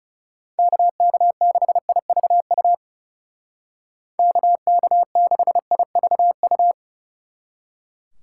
Morse code, heard as a keyed tone, about 35 wpm, 700 Hz. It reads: KK6IVU KK6IVU
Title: Speed Racing - USA 2x3 Call Signs 30wpm